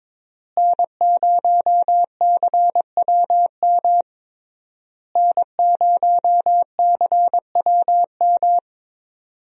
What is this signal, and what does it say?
Morse code, heard as a keyed tone, about 22 wpm, 700 Hz. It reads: N0CWM N0CWM
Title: USA Call Signs - 5 Characters - Repeated 3x - 22wpm